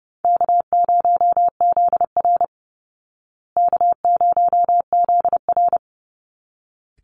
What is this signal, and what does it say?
Morse code, heard as a keyed tone, about 30 wpm, 700 Hz. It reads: K0ZR K0ZR
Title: Easy Call Signs Repeated 2x - 30wpm